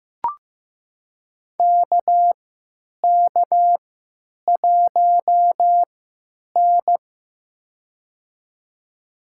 Morse code, heard as a keyed tone, about 15 wpm, spaced as for 9 wpm, 700 Hz. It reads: KK1N
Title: USA Call Signs - 3x Character Spacing - 15wpm